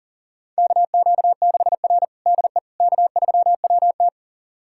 Morse code, heard as a keyed tone, about 40 wpm, 700 Hz. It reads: KQ6R DE K3WT
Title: USA Callsign DE Callsign - 4 Characters - 40wpm